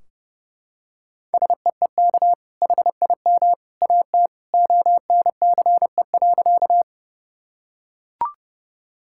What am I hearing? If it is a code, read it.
Morse code, heard as a keyed tone, about 30 wpm, 700 Hz. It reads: SEEK HIM AT ONCE.
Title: Sentences from Top 1600 Words 30wpm